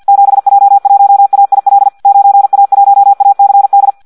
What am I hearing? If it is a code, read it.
Morse code, heard as a keyed tone, about 50 wpm, 800 Hz. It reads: 7J1AIL 9A1ACD
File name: Audio practica CW / qrq-tx-50 2